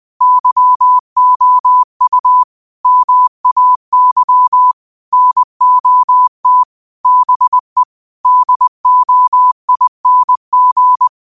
Morse code, heard as a keyed tone, about 20 wpm, 1000 Hz. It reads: YOU MAY NOT BE DOING
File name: qotdfiles / 20wpm